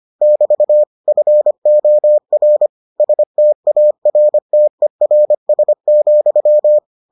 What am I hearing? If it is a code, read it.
Morse code, heard as a keyed tone, about 25 wpm, 600 Hz. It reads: <BT> FOR STARTERS,